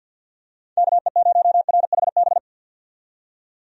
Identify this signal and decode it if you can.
Morse code, heard as a keyed tone, about 51 wpm, 700 Hz. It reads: KE0RHB
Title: USA Call Signs - 6 Characters - 50wpm